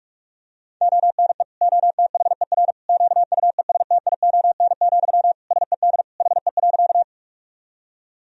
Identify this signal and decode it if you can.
Morse code, heard as a keyed tone, about 41 wpm, 700 Hz. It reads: ONE OTHER QUESTION, SED HE.
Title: Sentences from Top 400 Words and Abbreviations 45wpm